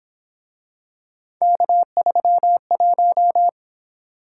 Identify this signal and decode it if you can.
Morse code, heard as a keyed tone, about 27 wpm, 700 Hz. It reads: K31
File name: Speed-Racing - Three Letter-Number Combinations - 22wpm